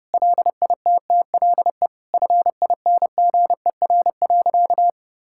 Morse code, heard as a keyed tone, about 30 wpm, 700 Hz. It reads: LITTLE FINGER.
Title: Sentences from Top 1600 Words 30wpm